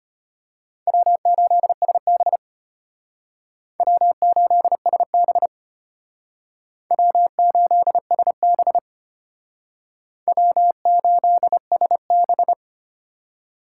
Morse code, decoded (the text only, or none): W8SB W8SB W8SB W8SB